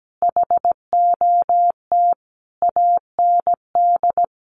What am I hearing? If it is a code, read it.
Morse code, heard as a keyed tone, about 17 wpm, 700 Hz. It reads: HOT AND
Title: Sentences from Top 1500 Words 17wpm